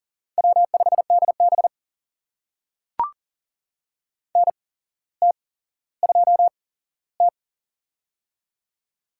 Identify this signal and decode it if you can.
Morse code, heard as a keyed tone, about 40 wpm, spaced as for 17 wpm, 700 Hz. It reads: W5DB NT2T
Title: USA Call Signs - 4 Characters - 8x Character Spacing - 40wpm